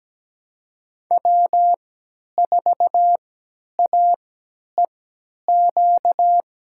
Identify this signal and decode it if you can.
Morse code, heard as a keyed tone, about 17 wpm, spaced as for 10 wpm, 700 Hz. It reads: W4AEQ
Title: USA Call Signs - 3x Character Spacing - 17wpm